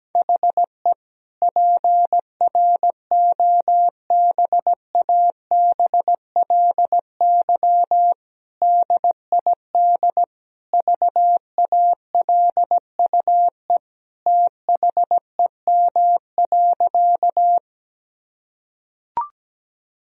Morse code, heard as a keyed tone, about 17 wpm, 700 Hz. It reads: HE PROBABLY DID VALUE THEM.
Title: Sentences from Top 1000 Words - Review Part 4 - 17wpm